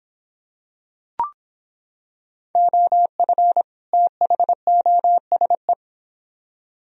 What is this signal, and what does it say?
Morse code, heard as a keyed tone, about 26 wpm, 700 Hz. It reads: OF THOSE